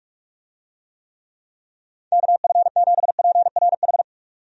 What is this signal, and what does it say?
Morse code, heard as a keyed tone, about 47 wpm, 700 Hz. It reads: KF7PRH
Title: USA Call Signs - 6 Characters - 45wpm